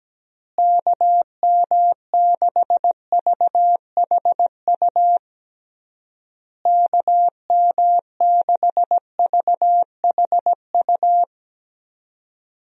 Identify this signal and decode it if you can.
Morse code, heard as a keyed tone, about 17 wpm, 700 Hz. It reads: KM6VHU KM6VHU
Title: USA Call Signs - 6 Characters - Repeated 3x - 17wpm